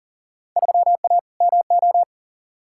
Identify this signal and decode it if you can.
Morse code, heard as a keyed tone, about 40 wpm, 700 Hz. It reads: 3A MO
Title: Field Day Exchanges - 40wpm